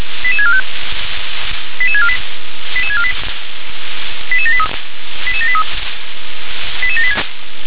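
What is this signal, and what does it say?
Five-tone selective call (ZVEI), heard as quick runs of five tones, tones of about 70 ms. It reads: 88645 88648 88648 88783 88783 88786
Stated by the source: ZVEI five-tone call received by SDR.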